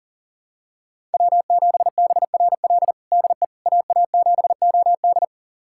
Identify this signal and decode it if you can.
Morse code, heard as a keyed tone, about 40 wpm, 700 Hz. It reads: W7BRL DE AA7OD
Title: USA Callsign DE Callsign - 5 Characters - 40wpm